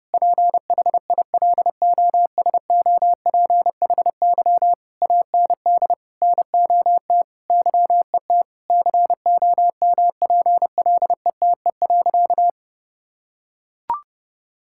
Morse code, heard as a keyed tone, about 30 wpm, 700 Hz. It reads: PHILOSOPHY AND NOT YET COMPLETE.
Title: Sentences from Top 1000 Words - Review Part 6 - 30wpm